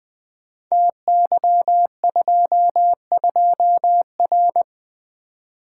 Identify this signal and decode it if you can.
Morse code, heard as a keyed tone, about 20 wpm, 700 Hz. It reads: TY22R
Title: Code Groups - 5 Letter:Numbers - 20wpm